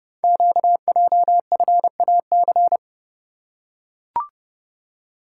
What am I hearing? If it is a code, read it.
Morse code, heard as a keyed tone, about 30 wpm, 700 Hz. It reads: QJFAC
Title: Code Groups - 5 Letters - 30wpm